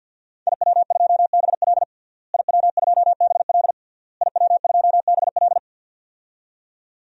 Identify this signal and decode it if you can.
Morse code, heard as a keyed tone, about 51 wpm, 700 Hz. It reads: IW2BL IW2BL IW2BL
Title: DXCC-100 Call Signs - 5 Characters - Repeated 3x - 50wpm